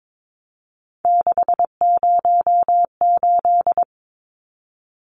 Morse code, heard as a keyed tone, about 22 wpm, 700 Hz. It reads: 608
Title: Numbers - 3-Digit 22wpm